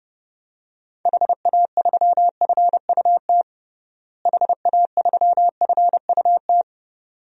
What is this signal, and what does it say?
Morse code, heard as a keyed tone, about 31 wpm, 700 Hz. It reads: HA3FUT HA3FUT
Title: DXCC-100 Call Signs - 6 Characters - Repeated 2x - 30wpm